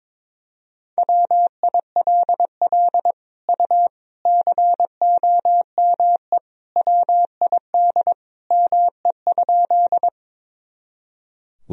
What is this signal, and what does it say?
Morse code, heard as a keyed tone, about 22 wpm, 700 Hz. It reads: WILL U COME WID ME?
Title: Sentences from Top 200 Words and Abbreviations 22wpm